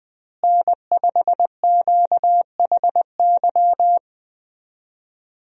Morse code, heard as a keyed tone, about 20 wpm, 700 Hz. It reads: N5QHY
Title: USA Call Signs - 5 Characters - Repeated 3x - 20wpm